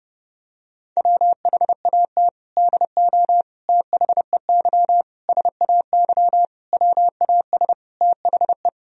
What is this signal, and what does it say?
Morse code, heard as a keyed tone, about 30 wpm, 700 Hz. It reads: WHAT DO THEY SAY WAS THE
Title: Sentences from Top 1000 Words - Review Part 1 - 30wpm